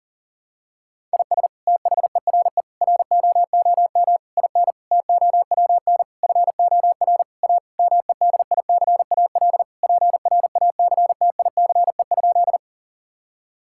Morse code, heard as a keyed tone, about 40 wpm, 700 Hz. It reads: IS THERE ROOM IN TOWN FOR A MEDICAL PRACTICE?
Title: Sentences from Top 1000 Words - Review Part 5 - 40wpm